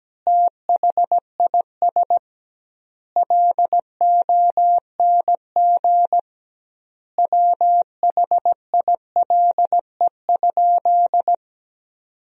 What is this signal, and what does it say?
Morse code, heard as a keyed tone, about 17 wpm, 700 Hz. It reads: THIS LONG WHILE?
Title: 2x Word-Spacing - Sentences from Top 300 Words - 17wpm